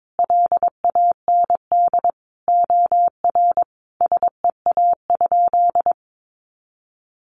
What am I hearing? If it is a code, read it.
Morse code, heard as a keyed tone, about 22 wpm, 700 Hz. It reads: LAND OR SEA?
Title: Sentences from Top 500 Words 22wpm